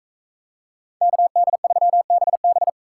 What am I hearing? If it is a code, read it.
Morse code, heard as a keyed tone, about 41 wpm, 700 Hz. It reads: KD3BB